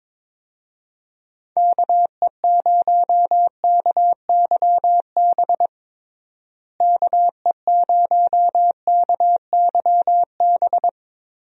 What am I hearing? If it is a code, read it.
Morse code, heard as a keyed tone, about 23 wpm, 700 Hz. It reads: KE0KYB KE0KYB
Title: USA Call Signs - 6 Characters - Repeated 3x - 22wpm